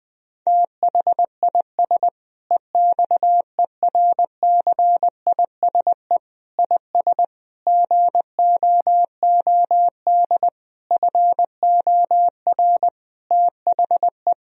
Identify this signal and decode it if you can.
Morse code, heard as a keyed tone, about 20 wpm, 700 Hz. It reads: THIS EXERCISE IS GOOD FOR THE
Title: Sentences from Top 1000 Words - Review Part 4 - 20wpm